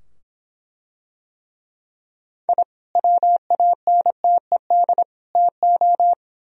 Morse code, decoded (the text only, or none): I WANTED TO